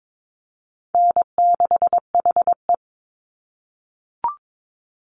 Morse code, heard as a keyed tone, about 22 wpm, 700 Hz. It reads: N6HE
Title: Easy Call Signs - 22wpm